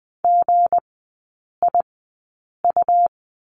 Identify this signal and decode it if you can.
Morse code, heard as a keyed tone, about 20 wpm, spaced as for 8 wpm, 700 Hz. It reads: GIU
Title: 198 Lesson 32e - Letter k - Three Characters 20wpm